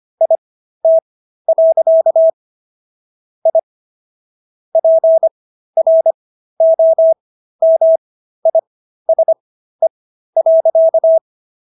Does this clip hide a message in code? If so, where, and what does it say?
Yes: Morse code from 0.2 to 11.2 s, heard as a keyed tone, about 25 wpm, spaced as for 13 wpm, 650 Hz. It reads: IT. I PROMISE.